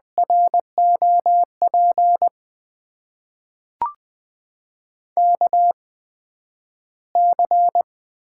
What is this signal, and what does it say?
Morse code, heard as a keyed tone, about 20 wpm, spaced as for 11 wpm, 700 Hz. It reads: ROP KC
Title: USA Call Signs - 6 Characters - 8x Character Spacing - 20wpm